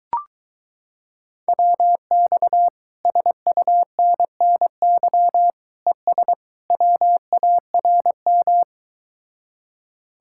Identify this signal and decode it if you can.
Morse code, heard as a keyed tone, about 23 wpm, 700 Hz. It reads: WX SUNNY ES WARM